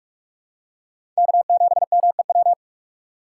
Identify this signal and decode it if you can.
Morse code, heard as a keyed tone, about 42 wpm, 700 Hz. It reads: K7MEW
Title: USA Call Signs - 6x Character Spacing - 45wpm